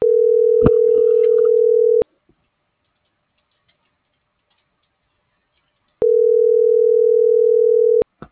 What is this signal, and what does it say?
Ringback tone (North America), 440 + 480 Hz.